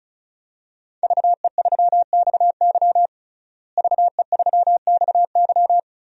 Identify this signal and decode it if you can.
Morse code, heard as a keyed tone, about 36 wpm, 700 Hz. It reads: VE3XY VE3XY
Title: DXCC-100 Call Signs - 5 Characters - Repeated 3x - 35wpm